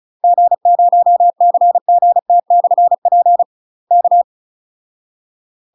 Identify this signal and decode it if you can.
Morse code, heard as a keyed tone, about 34 wpm, 700 Hz. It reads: G0CGT/P K
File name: UK MASTER QSOs TRAINING / 35WPM / UK MASTER QSOs 0042